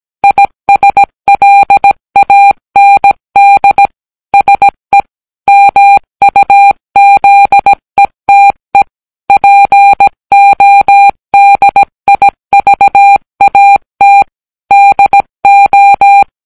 Morse code, heard as a keyed tone, about 17 wpm, 800 Hz. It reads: ISLAND SE MUZETE PODIVAT DO